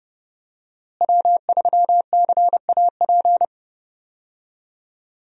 Morse code, heard as a keyed tone, about 30 wpm, 700 Hz. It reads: W3CAP
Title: USA Call Signs - 5 Characters - Repeated 3x - 30wpm